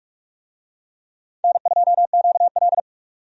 Morse code, heard as a keyed tone, about 46 wpm, 700 Hz. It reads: N2QL
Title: USA Call Signs - 4 Characters - 45wpm